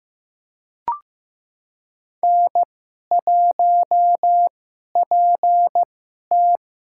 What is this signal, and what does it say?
Morse code, heard as a keyed tone, about 15 wpm, spaced as for 11 wpm, 700 Hz. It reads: N1PT
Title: USA Call Signs - 2x Character Spacing - 15wpm